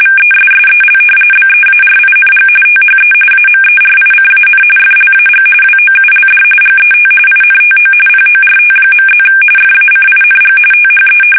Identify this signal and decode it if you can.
RTTY, about 45.45 baud. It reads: VESSEL BEARING THE AMERICAN FLAG WOULD GET INTO PORT.+ FACING THE
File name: fsk850 / battleships